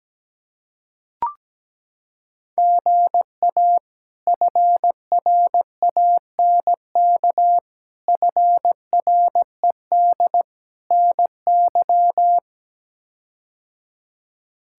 Morse code, heard as a keyed tone, about 17 wpm, 700 Hz. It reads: GA FRANK FRED NY
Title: K1USN SST QSO Practice - 17wpm